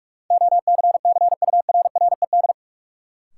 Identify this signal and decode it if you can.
Morse code, heard as a keyed tone, about 46 wpm, 700 Hz. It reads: OCCURRED
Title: Sets of 5 Words Encore - New Word Drill - 45wpm